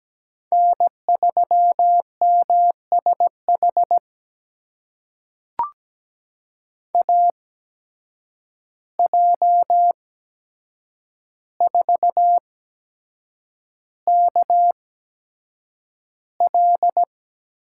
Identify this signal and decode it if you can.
Morse code, heard as a keyed tone, about 17 wpm, spaced as for 7 wpm, 700 Hz. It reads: N3MSH AJ4KL
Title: USA Call Signs - 5 Characters - 8x Character Spacing - 17wpm